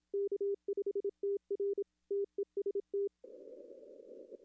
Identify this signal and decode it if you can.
Morse code, heard as a keyed tone, about 27 wpm, 400 Hz. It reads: K5TR TEST
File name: contests / arrl-160-k5tr-op-w5tn -0200